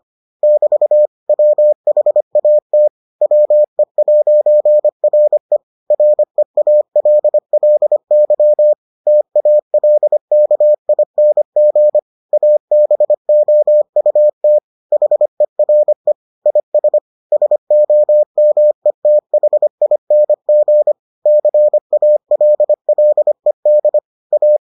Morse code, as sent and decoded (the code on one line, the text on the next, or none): -...- / .-- .... .- - / .-- . .----. .-. / .-. .- .-.. .-.. -.-- / - .- .-.. -.- .. -. --. / .- -... --- ..- - / .... .-. / .. ... / ... --- -- . - .... .. -. --. / -.-. .- .-.. .-.. -.. / .-
<BT> WHAT WE'RE REALLY TALKING ABOUT HERE IS SOMETHING CALLED A